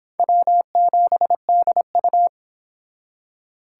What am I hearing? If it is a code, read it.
Morse code, heard as a keyed tone, about 25 wpm, 700 Hz. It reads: W7DU